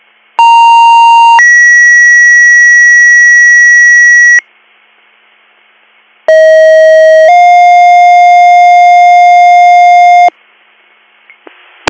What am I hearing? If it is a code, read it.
Two-tone page, heard as a one-second tone, then a three-second tone, about 930 then 1950 Hz, about 640 then 710 Hz. It is heard from 0.4 to 10.3 s.